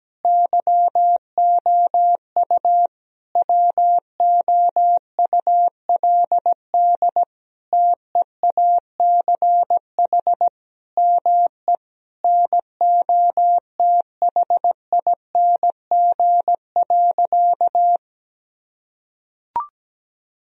Morse code, heard as a keyed tone, about 17 wpm, 700 Hz. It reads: YOU WOULD TEACH ME NOTHING.
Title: Sentences from Top 1000 Words - Review Part 2 - 17wpm